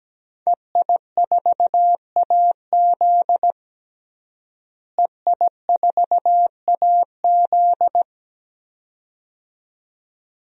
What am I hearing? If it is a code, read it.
Morse code, heard as a keyed tone, about 17 wpm, 700 Hz. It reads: EI4AZ EI4AZ
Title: DXCC-100 Call Signs - 5 Characters - Repeated 2x - 17wpm